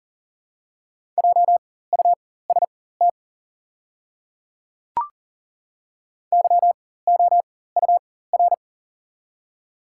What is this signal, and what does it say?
Morse code, heard as a keyed tone, about 40 wpm, spaced as for 19 wpm, 690 Hz. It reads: JUST YOUR
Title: 4x Character Spacing - Top 300 Words - 40wpm